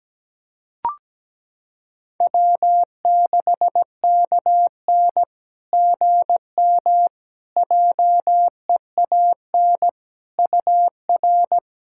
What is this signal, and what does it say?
Morse code, heard as a keyed tone, about 17 wpm, 700 Hz. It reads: W6KN GM JEAN UR
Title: SOTA QSO Practice 17wpm